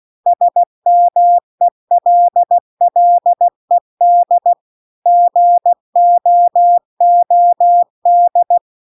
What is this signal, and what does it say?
Morse code, heard as a keyed tone, about 16 wpm, 700 Hz. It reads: SMELLED GOOD